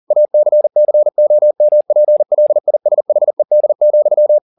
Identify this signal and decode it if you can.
Morse code, heard as a keyed tone, about 40 wpm, 600 Hz. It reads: ACCOMPLISHED,
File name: the old man and the sea 2 40wpm